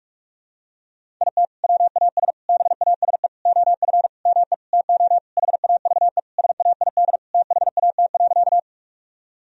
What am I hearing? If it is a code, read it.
Morse code, heard as a keyed tone, about 46 wpm, 700 Hz. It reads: IT WAS BASE OF ME TO HAVE SAID THAT.
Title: Sentences from Top 1000 Words - Review Part 1 - 45wpm